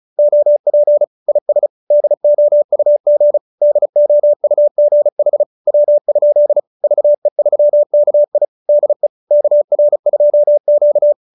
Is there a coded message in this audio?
Yes: Morse code from 0.2 to 11.1 s, heard as a keyed tone, about 34 wpm, 590 Hz. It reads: OP IS DOUG DOUG HW? VE3KI DE KR2Q